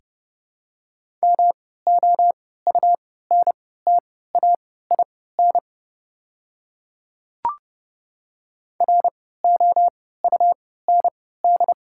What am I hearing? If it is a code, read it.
Morse code, heard as a keyed tone, about 30 wpm, spaced as for 17 wpm, 700 Hz. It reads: MOUNTAIN ROUND
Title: 3x Character Spacing - Top 300 Words - 30wpm